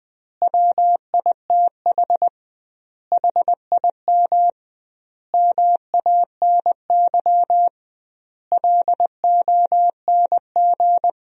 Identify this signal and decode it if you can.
Morse code, heard as a keyed tone, about 20 wpm, 700 Hz. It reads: WITH HIM MANY LONG
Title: 2x Word-Spacing - Sentences from Top 200 Words - 20wpm